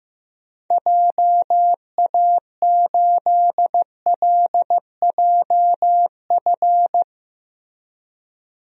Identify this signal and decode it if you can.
Morse code, heard as a keyed tone, about 15 wpm, 700 Hz. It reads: JA8LJF
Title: DXCC-100 Call Signs - 6 Characters - Repeated 3x - 15wpm